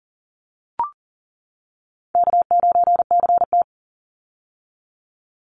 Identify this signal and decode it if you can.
Morse code, heard as a keyed tone, about 40 wpm, 700 Hz. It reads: K9CT
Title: Call Signs - 40wpm